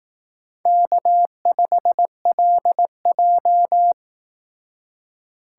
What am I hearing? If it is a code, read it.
Morse code, heard as a keyed tone, about 18 wpm, 700 Hz. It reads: K5LJ
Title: Speed Racing - USA 1x2 Call Signs 15wpm